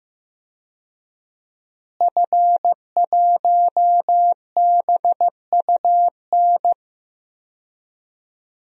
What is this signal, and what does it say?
Morse code, heard as a keyed tone, about 15 wpm, 700 Hz. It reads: F1BUN
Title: DXCC-100 Call Signs - 5 Characters - Repeated 3x - 15wpm